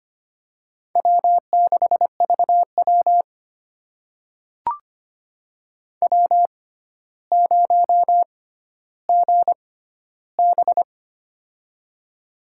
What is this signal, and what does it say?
Morse code, heard as a keyed tone, about 25 wpm, spaced as for 13 wpm, 700 Hz. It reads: W6VW W0GB
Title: USA Call Signs - 4 Characters - 6x Character Spacing - 25wpm